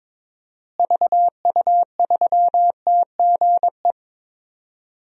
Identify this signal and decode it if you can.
Morse code, heard as a keyed tone, about 22 wpm, 700 Hz. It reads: VU3TGE
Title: DXCC-100 Call Signs - 6 Characters - Repeated 2x - 22wpm